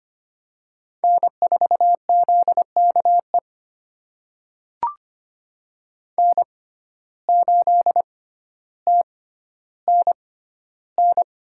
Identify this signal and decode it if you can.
Morse code, heard as a keyed tone, about 25 wpm, spaced as for 13 wpm, 700 Hz. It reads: N4ZKE N8TNN
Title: USA Call Signs - 5 Characters - 6x Character Spacing - 25wpm